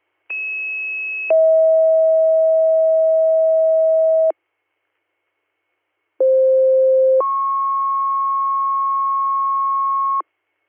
Two-tone page, heard as a one-second tone, then a three-second tone, about 2600 then 640 Hz, about 520 then 1090 Hz. It is heard from 0.3 to 10.2 s.